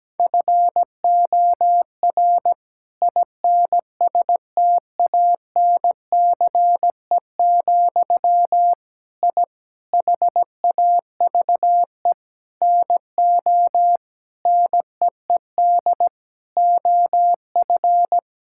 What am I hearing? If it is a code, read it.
Morse code, heard as a keyed tone, about 17 wpm, 700 Hz. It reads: FOR INSTANCE, I HAVE NO NEED OF
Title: Sentences from Top 1000 Words - Review Part 5 - 17wpm